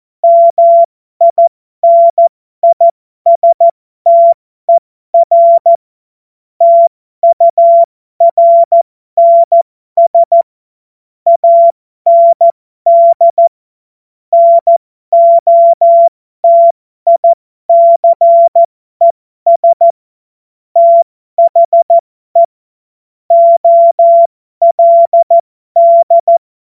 Morse code, heard as a keyed tone, about 14 wpm, spaced as for 12 wpm, 680 Hz. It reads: MINISTER TURNS AND NOTICES THE OLD
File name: Morsekurs / Witze